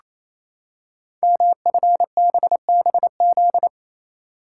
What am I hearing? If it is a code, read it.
Morse code, heard as a keyed tone, about 28 wpm, 700 Hz. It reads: MFBBZ
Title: Code Groups - 5 Letters - 28wpm